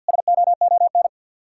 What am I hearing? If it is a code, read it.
Morse code, heard as a keyed tone, about 61 wpm, 700 Hz. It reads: SOON
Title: Speed-Racing Common QSO Phrases Encore - 50wpm